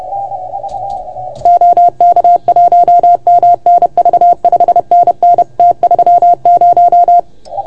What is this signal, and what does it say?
Morse code, heard as a keyed tone, about 30 wpm, 690 Hz. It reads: OK1MNV 5NN T30
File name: Archiv / CUC381 / OK1FGD CUC 381